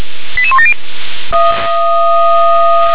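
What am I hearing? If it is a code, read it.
Five-tone selective call (ZVEI), heard as quick runs of five tones, tones of about 70 ms. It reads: 88170
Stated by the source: ZVEI five-tone selective call received on SDR.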